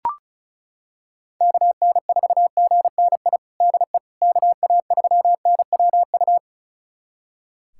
Morse code, heard as a keyed tone, about 34 wpm, 690 Hz. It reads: KN4GNI DE KA3NWU
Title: USA Callsign DE Callsign - 6 Characters - 35wpm